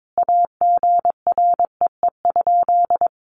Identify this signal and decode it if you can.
Morse code, heard as a keyed tone, about 22 wpm, 700 Hz. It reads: AGREE?
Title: Sentences from Top 1100 Words 22wpm